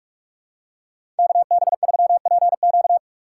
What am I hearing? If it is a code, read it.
Morse code, heard as a keyed tone, about 46 wpm, 700 Hz. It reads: KB3PQ
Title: USA Call Signs - 5 Characters - 6x Character Spacing - 45wpm